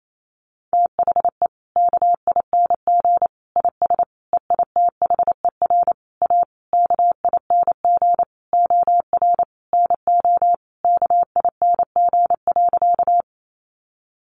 Morse code, heard as a keyed tone, about 28 wpm, 700 Hz. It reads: THE KING IS EITHER A KING OR NO KING.
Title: Sentences from Top 600 Words 28wpm